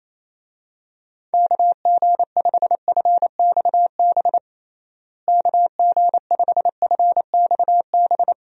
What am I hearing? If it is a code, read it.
Morse code, heard as a keyed tone, about 27 wpm, 700 Hz. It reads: KG5FXB KG5FXB
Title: USA Call Signs - 6 Characters - Repeated 3x - 28wpm